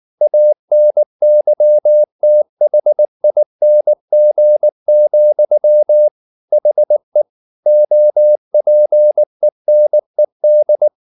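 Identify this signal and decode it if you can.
Morse code, heard as a keyed tone, about 19 wpm, 600 Hz. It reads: ANYTHING, HE OPENED
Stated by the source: Morse code at 19 wpm from CW med Gnister, reading Ken Follett's novel Never in English.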